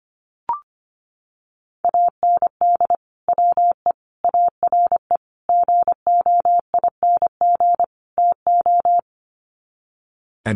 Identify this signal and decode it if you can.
Morse code, heard as a keyed tone, about 25 wpm, 700 Hz. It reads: AND WE ARE GOING TO
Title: Sets of 5 Words Encore2 25wpm